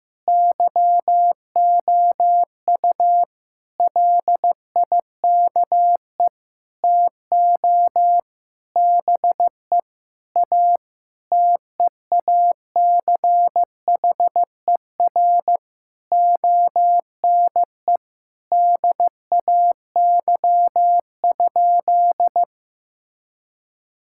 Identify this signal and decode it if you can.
Morse code, heard as a keyed tone, about 15 wpm, 700 Hz. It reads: YOU LIKE TO BE A TEACHER ONE DAY?
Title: Sentences from Top 1000 Words - Review Part 6 - 15wpm